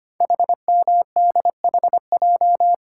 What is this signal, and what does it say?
Morse code, heard as a keyed tone, about 25 wpm, 700 Hz. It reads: HMDHJ
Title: Code Groups - 5 Letters - 25wpm